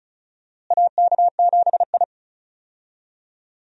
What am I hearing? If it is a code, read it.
Morse code, heard as a keyed tone, about 36 wpm, 700 Hz. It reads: AK7I
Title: USA Call Signs - 4 Characters - Repeated 2x - 35wpm